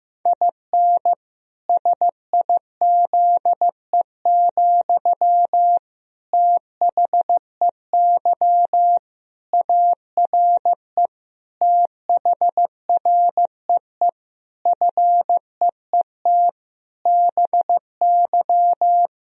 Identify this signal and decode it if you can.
Morse code, heard as a keyed tone, about 15 wpm, 700 Hz. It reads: IN SIZE, THEY ARE THREE FEET BY
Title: Sentences from Top 1000 Words - Review Part 2 - 15wpm